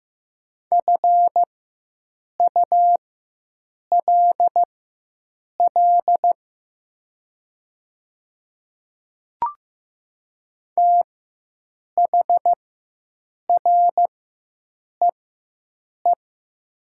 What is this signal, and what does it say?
Morse code, heard as a keyed tone, about 15 wpm, spaced as for 7 wpm, 700 Hz. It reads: FULL THREE
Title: 4x Character Spacing - Top 500 Words - 15wpm